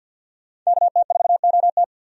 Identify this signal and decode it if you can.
Morse code, heard as a keyed tone, about 50 wpm, 700 Hz. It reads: KT4OT
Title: USA Call Signs - 5 Characters - 50wpm